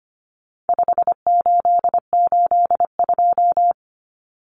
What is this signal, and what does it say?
Morse code, heard as a keyed tone, about 25 wpm, 700 Hz. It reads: 5882
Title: Numbers - 4-Digit 25wpm